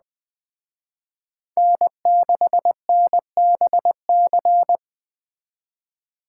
Morse code, heard as a keyed tone, about 21 wpm, 700 Hz. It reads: N6NBC